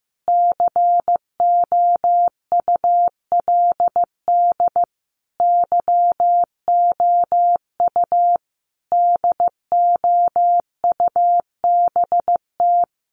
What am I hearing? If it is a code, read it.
Morse code, heard as a keyed tone, about 15 wpm, 700 Hz. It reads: COULD YOU DOUBT
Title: Sentences from Top 1500 Words 15wpm